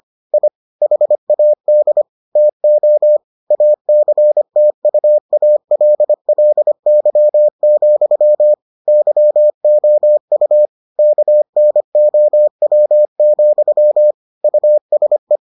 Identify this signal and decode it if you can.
Morse code, heard as a keyed tone, about 25 wpm, 600 Hz. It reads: I HAD TO ACTUALLY, YOU KNOW, USE